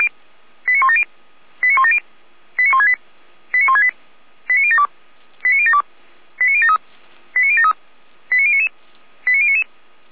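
Five-tone selective call (ZVEI), heard as quick runs of five tones, tones of about 70 ms. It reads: 89170 89170 89168 89168 89072 89072 89073 89073 89099 89099
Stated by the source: Received on SDR, ZVEI five-tone code 89560, 455 channel.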